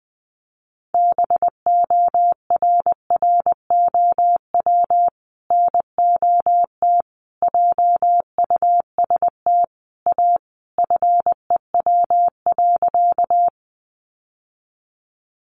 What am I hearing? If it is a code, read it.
Morse code, heard as a keyed tone, about 20 wpm, 700 Hz. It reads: BORROW NOT JUST A FEW.
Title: Sentences from Top 1500 Words 20wpm